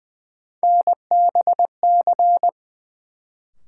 Morse code, heard as a keyed tone, about 20 wpm, 700 Hz. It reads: NBC
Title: Sets of 5 Words Encore - New Word Drill - 20wpm